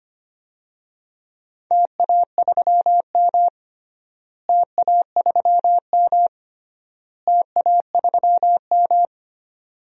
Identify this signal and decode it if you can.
Morse code, heard as a keyed tone, about 25 wpm, 700 Hz. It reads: TA3M TA3M TA3M
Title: DXCC-100 Call Signs - 4 Characters - Repeated 3x - 25wpm